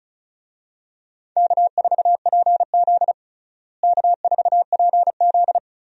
Morse code, heard as a keyed tone, about 36 wpm, 700 Hz. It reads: K4PZ K4PZ
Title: USA Call Signs - 4 Characters - Repeated 3x - 35wpm